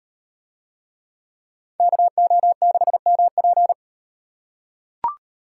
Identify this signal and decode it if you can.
Morse code, heard as a keyed tone, about 38 wpm, 700 Hz. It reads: KO6MP